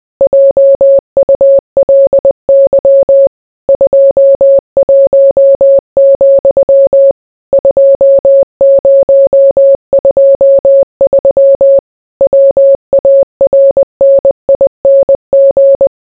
Morse code, heard as a keyed tone, about 20 wpm, 550 Hz. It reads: JULY 21, 2023 WARNING